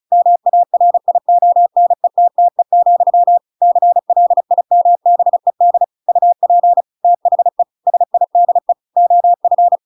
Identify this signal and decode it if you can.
Morse code, heard as a keyed tone, about 34 wpm, 700 Hz. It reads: MARIONETTE, CLIMBED UP THE SIDE OF
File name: Pinocchio / Pinocchio ch7 35 35 ik1hge